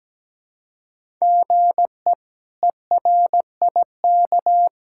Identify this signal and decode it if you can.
Morse code, heard as a keyed tone, about 17 wpm, 700 Hz. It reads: GE ERIK
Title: K1USN SST QSO Practice - 17wpm